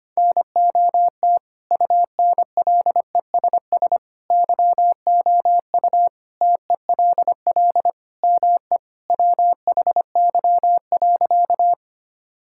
Morse code, heard as a keyed tone, about 25 wpm, 700 Hz. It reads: NOT UNLESS YOU TELL ME WHY.
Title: Sentences from Top 1000 Words - Review Part 6 - 25wpm